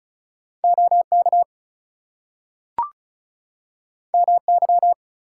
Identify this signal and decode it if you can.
Morse code, heard as a keyed tone, about 36 wpm, 700 Hz. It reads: OK MY
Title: Two Letter Words - 35wpm